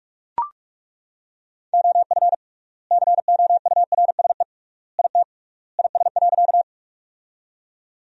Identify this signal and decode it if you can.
Morse code, heard as a keyed tone, about 41 wpm, 700 Hz. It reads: OF COURSE IT IS.
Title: 3x Word-Spacing - Sentences from Top 500 Words - 45wpm